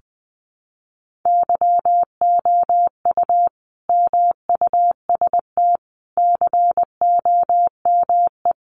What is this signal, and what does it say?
Morse code, heard as a keyed tone, about 20 wpm, 700 Hz. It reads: YOU MUST COME
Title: Sentences from Top 200 Words 20wpm